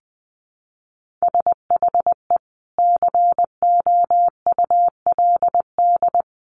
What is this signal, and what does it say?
Morse code, heard as a keyed tone, about 19 wpm, 700 Hz. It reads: SHE COULD
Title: Sentences from Top 700 Words 20wpm